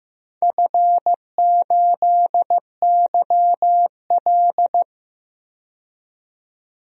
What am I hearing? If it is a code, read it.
Morse code, heard as a keyed tone, about 15 wpm, 700 Hz. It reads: F8YL
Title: DXCC-100 Call Signs - 4 Characters - 15wpm